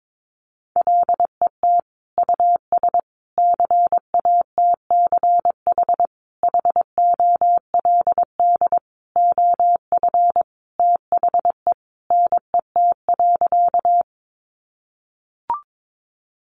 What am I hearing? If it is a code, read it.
Morse code, heard as a keyed tone, about 22 wpm, 700 Hz. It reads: LET US CATCH HOLD OF THE NET.
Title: Sentences from Top 1500 Words 22wpm